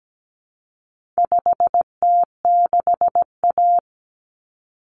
Morse code, heard as a keyed tone, about 17 wpm, 700 Hz. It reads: 5T6A
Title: Common Cut Numbers - 4-Digit 17wpm